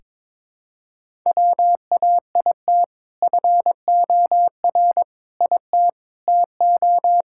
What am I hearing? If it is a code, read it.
Morse code, heard as a keyed tone, about 22 wpm, 700 Hz. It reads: WAIT FOR IT TO
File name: practice / 22wpm / Sentences with 2-Word Phrases - 22wpm